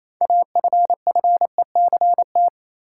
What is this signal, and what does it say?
Morse code, heard as a keyed tone, about 28 wpm, 700 Hz. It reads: AFFECT
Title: Sentences from Top 900 Words - New Word Drill - 28wpm